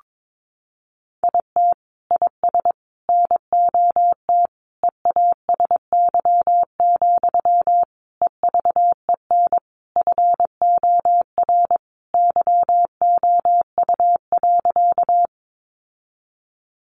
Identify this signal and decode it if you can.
Morse code, heard as a keyed tone, about 22 wpm, 700 Hz. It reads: IT IS NOT EASY, EVEN FOR YOU.
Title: Sentences from Top 700 Words 22wpm